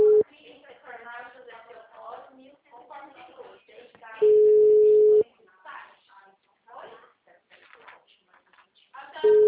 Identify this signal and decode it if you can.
Ringback tone (Europe, 425 Hz), with speech.